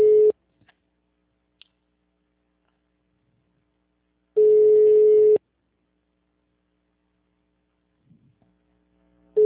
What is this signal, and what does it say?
Ringback tone (Europe, 425 Hz).